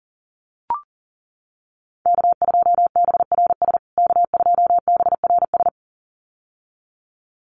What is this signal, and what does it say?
Morse code, heard as a keyed tone, about 40 wpm, 700 Hz. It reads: K2BRS K2BRS
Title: USA Call Signs Repeated 2x 40wpm